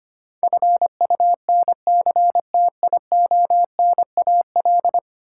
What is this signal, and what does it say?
Morse code, heard as a keyed tone, about 25 wpm, 700 Hz. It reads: FUNCTIONAL
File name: Ten Letter Words - 25wpm